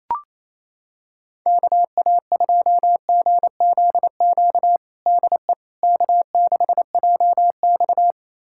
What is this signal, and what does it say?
Morse code, heard as a keyed tone, about 28 wpm, 700 Hz. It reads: KA2GZQ DE K6JX
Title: USA Callsign DE Callsign 28wpm